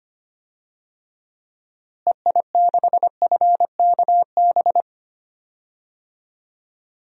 Morse code, heard as a keyed tone, about 25 wpm, 700 Hz. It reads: EI6FKB
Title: DXCC-100 Call Signs - 6 Characters - Repeated 2x - 25wpm